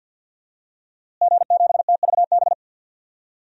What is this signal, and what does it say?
Morse code, heard as a keyed tone, about 53 wpm, 700 Hz. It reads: G7TVB
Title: DXCC-100 Call Signs - 5 Characters - Repeated 3x - 50wpm